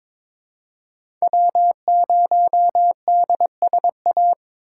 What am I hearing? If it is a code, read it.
Morse code, heard as a keyed tone, about 23 wpm, 700 Hz. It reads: W0DSA
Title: USA Call Signs - 5 Characters - 3x Character Spacing - 22wpm